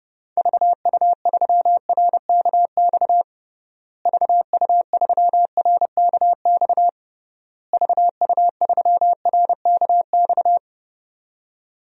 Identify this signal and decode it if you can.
Morse code, heard as a keyed tone, about 32 wpm, 700 Hz. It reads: VU3RKX VU3RKX VU3RKX
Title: DXCC-100 Call Signs - 6 Characters - Repeated 3x - 30wpm